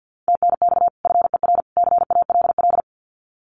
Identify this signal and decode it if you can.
Morse code, heard as a keyed tone, about 48 wpm, 700 Hz. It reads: TNX FER CALL
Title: Common QSO Phrases 50wpm